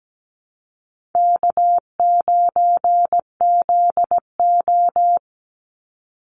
Morse code, heard as a keyed tone, about 17 wpm, 700 Hz. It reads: K9ZO
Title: Easy Call Signs Repeated 2x - 17wpm